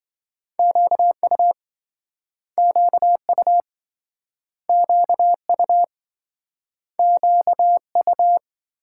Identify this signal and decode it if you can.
Morse code, heard as a keyed tone, about 27 wpm, 700 Hz. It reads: QU QU QU QU